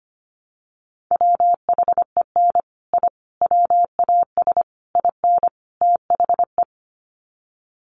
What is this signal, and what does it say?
Morse code, heard as a keyed tone, about 25 wpm, 700 Hz. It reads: WHEN I WAS IN THE
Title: Sets of 5 Words Encore2 25wpm